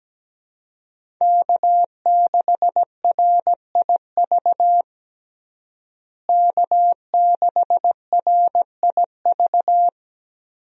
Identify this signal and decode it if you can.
Morse code, heard as a keyed tone, about 17 wpm, 700 Hz. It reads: K6RIV K6RIV
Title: USA Call Signs - 5 Characters - Repeated 3x - 17wpm